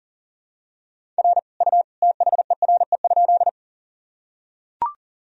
Morse code, heard as a keyed tone, about 40 wpm, 700 Hz. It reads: R U THERE?
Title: Sentences from Top 100 Words and Abbreviations 40wpm